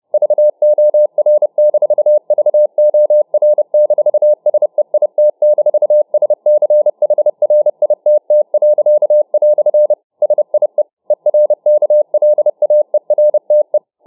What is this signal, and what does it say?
Morse code, heard as a keyed tone, about 30 wpm, 600 Hz. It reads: VOR-VOR-SEIT-SCHRITT." SIE ERKLAERTE